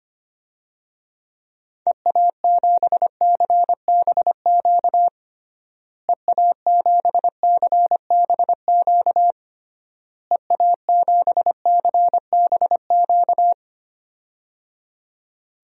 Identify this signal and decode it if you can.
Morse code, heard as a keyed tone, about 25 wpm, 700 Hz. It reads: EA7CBQ EA7CBQ EA7CBQ
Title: DXCC-100 Call Signs - 6 Characters - Repeated 3x - 25wpm